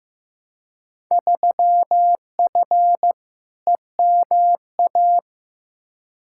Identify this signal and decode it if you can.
Morse code, heard as a keyed tone, about 15 wpm, 700 Hz. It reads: 3F EMA
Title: Field Day Exchanges Encore - 15wpm